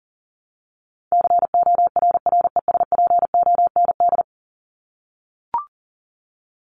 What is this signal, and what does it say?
Morse code, heard as a keyed tone, about 40 wpm, 700 Hz. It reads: CORRESPOND
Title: ING Suffix - Top 100 Words with Base Words 40wpm